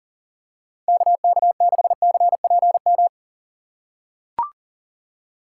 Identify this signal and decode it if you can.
Morse code, heard as a keyed tone, about 39 wpm, 700 Hz. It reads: KK6CPM
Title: USA Call Signs - 6 Characters - 4x Character Spacing - 40wpm